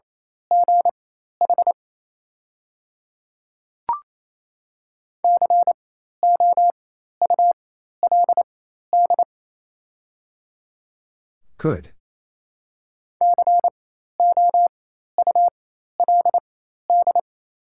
Morse code, heard as a keyed tone, about 28 wpm, spaced as for 13 wpm, 700 Hz. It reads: GH COULD COULD
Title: 4x Character Spacing - Top 200 Words - 28wpm